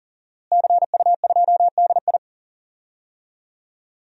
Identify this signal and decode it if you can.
Morse code, heard as a keyed tone, about 40 wpm, 700 Hz. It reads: CU2DI
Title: DXCC-100 Call Signs - 5 Characters - Repeated 2x - 40wpm